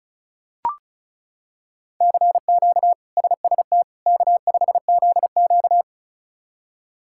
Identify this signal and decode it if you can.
Morse code, heard as a keyed tone, about 36 wpm, 700 Hz. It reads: CQ SST K5ZQ
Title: K1USN SST QSO Practice - 35wpm